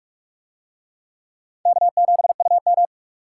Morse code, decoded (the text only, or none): K7UM